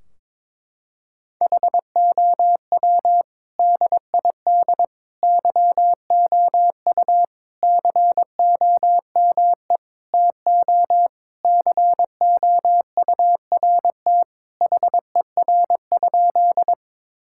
Morse code, heard as a keyed tone, about 22 wpm, 700 Hz. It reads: HOW DID YOU COME TO COURT HER?
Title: Sentences from Top 1600 Words 22wpm